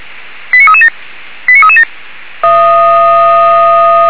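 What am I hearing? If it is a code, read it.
Five-tone selective call (ZVEI), heard as quick runs of five tones, tones of about 70 ms. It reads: 89337 89337
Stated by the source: Five-tone ZVEI selective call 89315, received by SDR, fire brigade Orlamünde.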